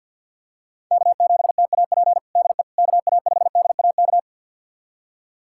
Morse code, heard as a keyed tone, about 47 wpm, 700 Hz. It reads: K7TAP DE KA5DAK
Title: USA Callsign DE Callsign 50wpm